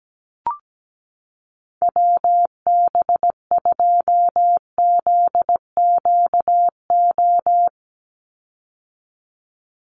Morse code, heard as a keyed tone, about 17 wpm, 700 Hz. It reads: WB2ZQO
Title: USA Call Signs 17wpm